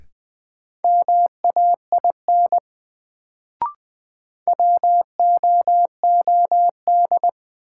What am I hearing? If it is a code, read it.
Morse code, heard as a keyed tone, about 21 wpm, 700 Hz. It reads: MAIN WOOD
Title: Sentences from Top 1000 Words - Review Part 2 - New Word Drill - 20wpm